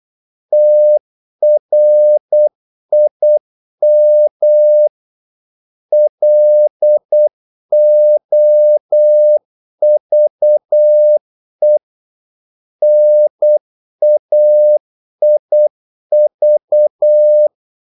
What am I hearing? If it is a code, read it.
Morse code, heard as a keyed tone, about 8 wpm, 600 Hz. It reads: TRIM LOVE NAIV
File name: Korte ord DK 8wpm